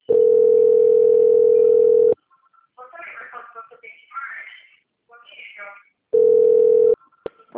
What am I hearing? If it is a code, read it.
Ringback tone (North America), 440 + 480 Hz, with speech.
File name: out- - - - - .39985